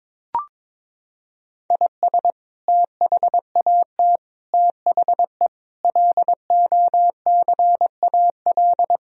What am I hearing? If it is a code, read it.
Morse code, heard as a keyed tone, about 22 wpm, 700 Hz. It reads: IS THAT THE LOCAL
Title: Sentences from Top 1000 Words - Review Part 3 - 22wpm